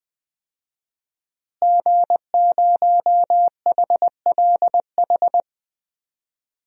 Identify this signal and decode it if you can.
Morse code, heard as a keyed tone, about 20 wpm, 700 Hz. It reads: G0HLH
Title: DXCC-100 Call Signs - 5 Characters - Repeated 3x - 20wpm